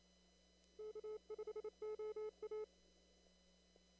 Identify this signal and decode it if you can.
Morse code, heard as a keyed tone, about 26 wpm, 430 Hz. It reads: K5OA
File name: CW / kt5j wpx cw-op-k5pi